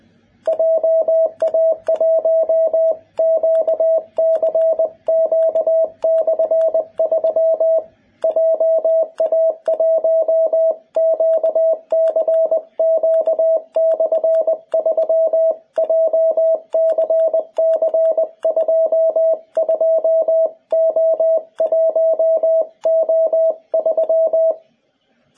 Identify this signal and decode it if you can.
Morse code, heard as a keyed tone, about 20 wpm, 650 Hz, with radio noise. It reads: JA1QCQ/3 JA1QCQ/3 JCC22O1O3
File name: jh8cbh / 2021cw